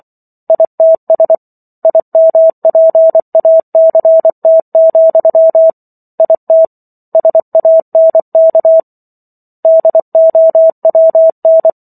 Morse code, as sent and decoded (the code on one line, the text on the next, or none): .. - ... / .. -- .--. .- -.-. - --..-- / .. - / ... .- -. -.- / -.. --- .-- -.
ITS IMPACT, IT SANK DOWN